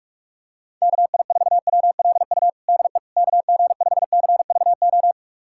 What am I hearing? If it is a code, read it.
Morse code, heard as a keyed tone, about 41 wpm, 700 Hz. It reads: KI4WLU DE KG5CVO
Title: USA Callsign DE Callsign - 6 Characters - 45wpm